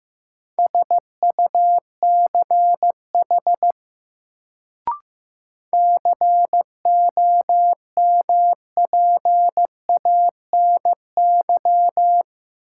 Morse code, heard as a keyed tone, about 15 wpm, 700 Hz. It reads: SUCH COMPANY
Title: Sets of 2 Words - New Word Drill - 15wpm